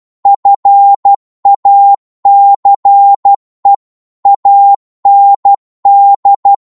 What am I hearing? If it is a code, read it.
Morse code, heard as a keyed tone, about 12 wpm, 800 Hz. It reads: FACE AND
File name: Morse / psg-12wpm